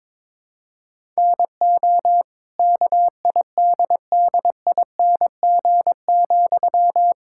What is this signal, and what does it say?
Morse code, heard as a keyed tone, about 22 wpm, 700 Hz. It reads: NO KIDDING,
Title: Pangram Challenge v2 - 22wpm